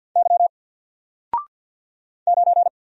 Morse code, heard as a keyed tone, about 50 wpm, 700 Hz. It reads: Y 9
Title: Single Letter-Number 50wpm